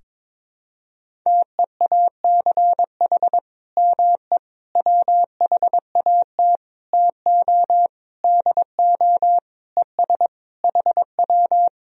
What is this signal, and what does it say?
Morse code, heard as a keyed tone, about 22 wpm, 700 Hz. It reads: TEACH ME WHAT TO DO ES HW
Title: Sentences from Top 500 Words and Abbreviations 22wpm